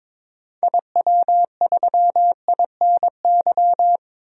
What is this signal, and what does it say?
Morse code, heard as a keyed tone, about 22 wpm, 700 Hz. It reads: IW3INY
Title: DXCC-100 Call Signs - 6 Characters - 22wpm